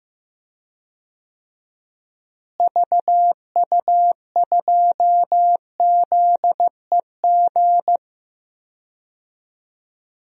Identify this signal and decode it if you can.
Morse code, heard as a keyed tone, about 15 wpm, 700 Hz. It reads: VU2ZEG
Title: DXCC-100 Call Signs - 6 Characters - Repeated 3x - 15wpm